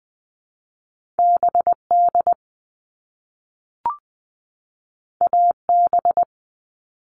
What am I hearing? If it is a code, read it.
Morse code, heard as a keyed tone, about 20 wpm, 700 Hz. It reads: BD AB
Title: All Cut Numbers - 2-Digit 20wpm